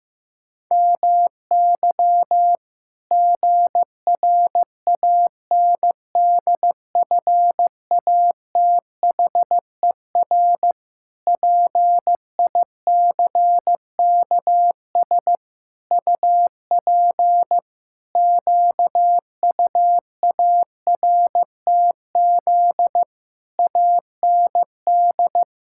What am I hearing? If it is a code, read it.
Morse code, heard as a keyed tone, about 16 wpm, 700 Hz. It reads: MY GRANDFATHER PICKS UP QUARTZ AND
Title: Pangram Challenge v2 - 15wpm